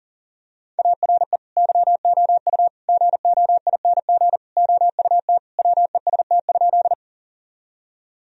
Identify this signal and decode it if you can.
Morse code, heard as a keyed tone, about 40 wpm, 700 Hz. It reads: ARE YOU GOING OUT WEST?
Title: Sentences from Top 1000 Words - Review Part 2 - 40wpm